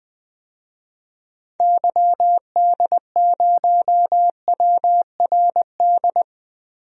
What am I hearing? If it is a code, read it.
Morse code, heard as a keyed tone, about 20 wpm, 700 Hz. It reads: YD0WRD
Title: DXCC-100 Call Signs - 6 Characters - 20wpm